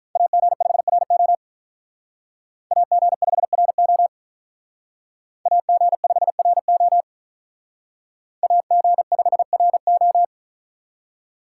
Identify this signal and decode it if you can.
Morse code, heard as a keyed tone, about 45 wpm, 700 Hz. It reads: AG5RO AG5RO AG5RO AG5RO